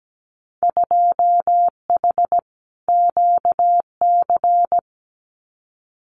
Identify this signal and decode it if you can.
Morse code, heard as a keyed tone, about 17 wpm, 700 Hz. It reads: 2H QC
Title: Winter Field Day Exchanges Encore 17wpm